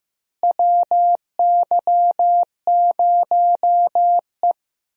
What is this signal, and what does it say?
Morse code, heard as a keyed tone, about 16 wpm, 700 Hz. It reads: WY0E
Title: USA Call Signs - 4 Characters - Repeated 3x - 15wpm